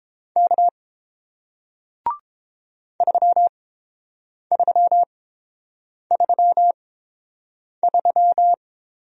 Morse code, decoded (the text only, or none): K 3333